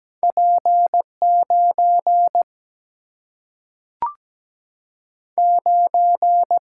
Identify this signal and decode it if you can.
Morse code, heard as a keyed tone, about 17 wpm, 700 Hz. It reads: P9 9
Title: Single-Letter Single-Number 17wpm